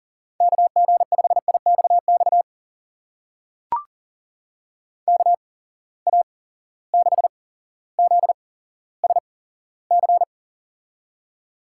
Morse code, heard as a keyed tone, about 40 wpm, spaced as for 17 wpm, 700 Hz. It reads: KG5IXX KA6ZSC
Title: USA Call Signs - 6 Characters - 8x Character Spacing - 40wpm